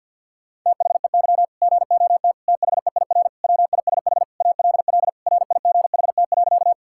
Morse code, heard as a keyed tone, about 51 wpm, 700 Hz. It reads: THEY GOT THEIR WISH ALL RIGHT.
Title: Sentences from Top 1600 Words 50wpm